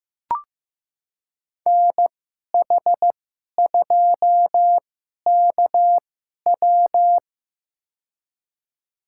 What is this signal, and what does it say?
Morse code, heard as a keyed tone, about 15 wpm, spaced as for 11 wpm, 700 Hz. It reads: NH2KW
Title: USA Call Signs - 2x Character Spacing - 15wpm